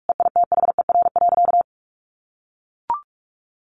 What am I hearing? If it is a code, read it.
Morse code, heard as a keyed tone, about 44 wpm, 700 Hz. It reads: EITHER.
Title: Sentences from Top 1200 Words 45wpm